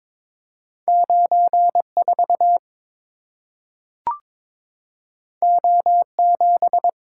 Morse code, heard as a keyed tone, about 22 wpm, 700 Hz. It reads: 94 O7
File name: practice / 22wpm / Two Letter-Number Combinations - 22wpm